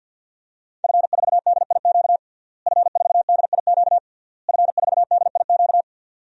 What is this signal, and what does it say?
Morse code, heard as a keyed tone, about 48 wpm, 700 Hz. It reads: F4DIQ F4DIQ F4DIQ
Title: DXCC-100 Call Signs - 5 Characters - Repeated 3x - 50wpm